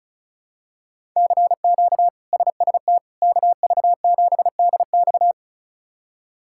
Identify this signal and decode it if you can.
Morse code, heard as a keyed tone, about 36 wpm, 700 Hz. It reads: CQ SST KV7DX
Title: K1USN SST QSO Practice - 35wpm